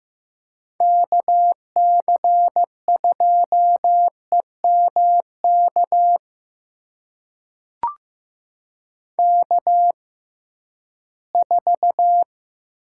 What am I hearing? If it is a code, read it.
Morse code, heard as a keyed tone, about 15 wpm, spaced as for 11 wpm, 700 Hz. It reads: KC2EMK K4
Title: USA Call Signs - 6x Character Spacing - 15wpm